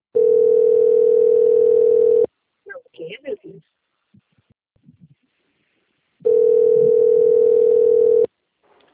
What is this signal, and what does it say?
Ringback tone (North America), 440 + 480 Hz, with speech.
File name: out- - - - - .9324